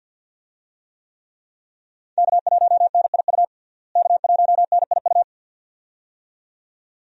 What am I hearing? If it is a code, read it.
Morse code, heard as a keyed tone, about 46 wpm, 700 Hz. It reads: K1NIU K1NIU
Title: USA Call Signs - 5 Characters - Repeated 2x - 50wpm